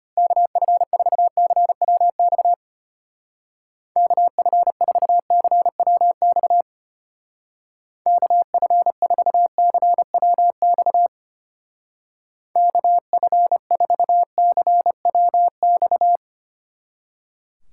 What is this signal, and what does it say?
Morse code, heard as a keyed tone, about 31 wpm, 700 Hz. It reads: KF4CWX KF4CWX KF4CWX KF4CWX
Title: Speed Racing - USA 2x3 Call Signs 25wpm